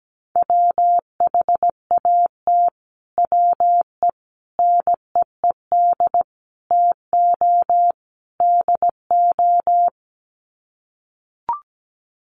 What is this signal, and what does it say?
Morse code, heard as a keyed tone, about 17 wpm, 700 Hz. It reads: WHAT WE NEED TO DO
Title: Sets of 5 Words Encore 17wpm